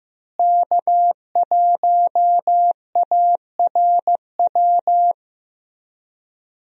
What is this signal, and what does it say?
Morse code, heard as a keyed tone, about 15 wpm, 700 Hz. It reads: K1ARW
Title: USA Call Signs - 5 Characters - Repeated 3x - 15wpm